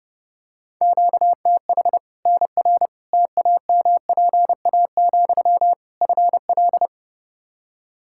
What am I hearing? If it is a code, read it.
Morse code, heard as a keyed tone, about 30 wpm, 700 Hz. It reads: QTH NR TAMPA, FL
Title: Practice QSO - Exchange 1 - Beginning - 30wpm